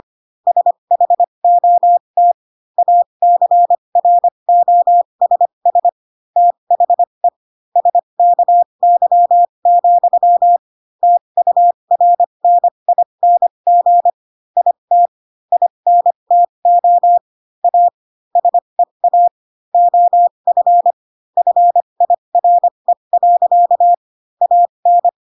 Morse code, heard as a keyed tone, about 25 wpm, spaced as for 22 wpm, 700 Hz. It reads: SHOT ACROSS THE SKY, TURNING IT INTO A SEA OF FIRE. AN
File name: Pinocchio / Pinocchio ch6 18 25 ik1hge